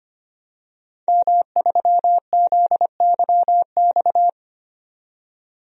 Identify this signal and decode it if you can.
Morse code, heard as a keyed tone, about 25 wpm, 700 Hz. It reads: M3ZYX
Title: DXCC-100 Call Signs - 5 Characters - 25wpm